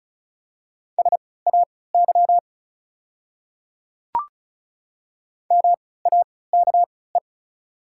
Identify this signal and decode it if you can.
Morse code, heard as a keyed tone, about 36 wpm, spaced as for 20 wpm, 690 Hz. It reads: SAY MAKE
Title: 3x Character Spacing - Top 200 Words - 35wpm